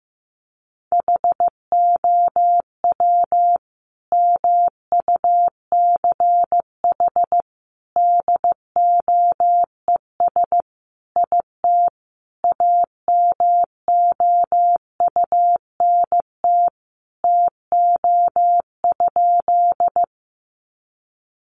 Morse code, heard as a keyed tone, about 15 wpm, 700 Hz. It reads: HOW MUCH DOES IT AMOUNT TO?
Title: Sentences from Top 700 Words 15wpm